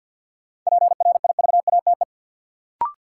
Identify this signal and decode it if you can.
Morse code, heard as a keyed tone, about 52 wpm, 700 Hz. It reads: PRIVATE
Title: Sentences from Top 1000 Words - New Word Drill - 50wpm